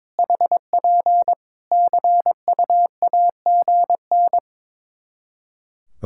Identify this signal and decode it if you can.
Morse code, heard as a keyed tone, about 22 wpm, 700 Hz. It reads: HP CUAGN
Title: K1USN SST QSO Practice 22wpm